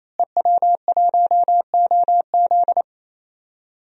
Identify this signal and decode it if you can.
Morse code, heard as a keyed tone, about 28 wpm, 700 Hz. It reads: EW1OZ
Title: DXCC-100 Call Signs - 5 Characters - Repeated 3x - 28wpm